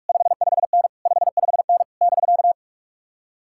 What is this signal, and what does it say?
Morse code, heard as a keyed tone, about 64 wpm, spaced as for 53 wpm, 700 Hz. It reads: 55N 55N <BK>
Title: Speed-Racing Common QSO Phrases Encore - 45wpm